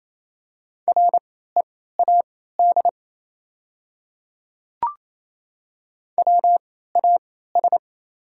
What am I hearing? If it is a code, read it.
Morse code, heard as a keyed tone, about 28 wpm, spaced as for 16 wpm, 690 Hz. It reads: READ WAS
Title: 3x Character Spacing - Top 500 Words - 28wpm